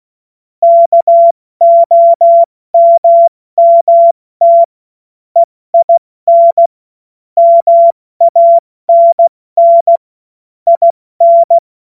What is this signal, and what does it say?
Morse code, heard as a keyed tone, about 16 wpm, 680 Hz. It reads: KOMMT EIN MANN IN
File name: Morsekurs / Witze / Witz 12 März 2024 70 F80